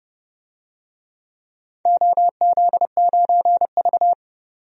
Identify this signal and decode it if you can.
Morse code, heard as a keyed tone, about 30 wpm, 700 Hz. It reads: OZ9V
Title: DXCC-100 Call Signs - 4 Characters - 30wpm